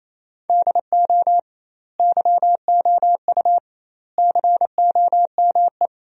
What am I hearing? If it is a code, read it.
Morse code, heard as a keyed tone, about 28 wpm, 700 Hz. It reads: DO YOU COME
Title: 2x Word-Spacing - Sentences from Top 200 Words - 28wpm